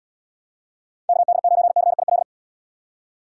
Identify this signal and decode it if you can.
Morse code, heard as a keyed tone, about 77 wpm, 700 Hz. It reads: KD9CIL